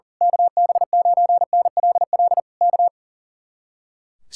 Morse code, heard as a keyed tone, about 38 wpm, 700 Hz. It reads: KB9NLL K
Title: Speed-Racing Common QSO Phrases - 40wpm